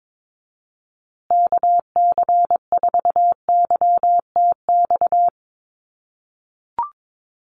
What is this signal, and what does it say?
Morse code, heard as a keyed tone, about 22 wpm, 700 Hz. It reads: KC4YTX
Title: USA Call Signs Repeated 3x 22wpm